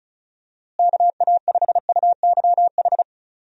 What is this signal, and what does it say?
Morse code, heard as a keyed tone, about 34 wpm, 700 Hz. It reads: KA5UYH
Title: USA Call Signs - 6 Characters - Repeated 3x - 35wpm